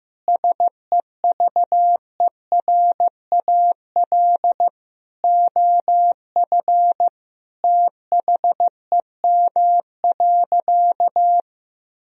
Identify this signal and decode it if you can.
Morse code, heard as a keyed tone, about 15 wpm, 700 Hz. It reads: SEVERAL OF THEM.
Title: Sentences from Top 1000 Words - Review Part 2 - 15wpm